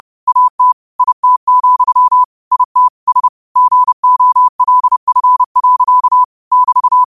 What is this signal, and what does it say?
Morse code, heard as a keyed tone, about 30 wpm, 1000 Hz. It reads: AT IT, IT S GOLF. <BT>
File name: qotdfiles / 30wpm